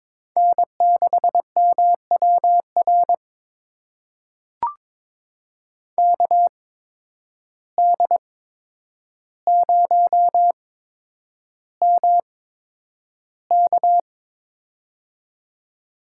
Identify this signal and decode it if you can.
Morse code, heard as a keyed tone, about 22 wpm, spaced as for 9 wpm, 700 Hz. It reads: N6MWR KD0MK
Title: USA Call Signs - 5 Characters - 8x Character Spacing - 22wpm